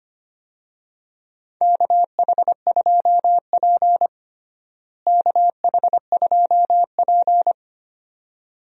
Morse code, heard as a keyed tone, about 25 wpm, 700 Hz. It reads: KH2P KH2P
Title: USA Call Signs - 4 Characters - Repeated 2x - 25wpm